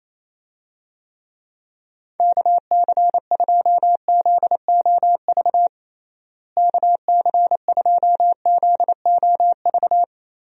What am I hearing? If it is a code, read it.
Morse code, heard as a keyed tone, about 28 wpm, 700 Hz. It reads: KC2ZOV KC2ZOV
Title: USA Call Signs - 6 Characters - Repeated 2x - 28wpm